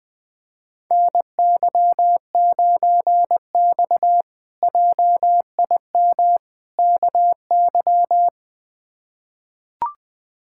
Morse code, heard as a keyed tone, about 20 wpm, 700 Hz. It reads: NY9X JIM KY
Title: K1USN SST QSO Practice - 20wpm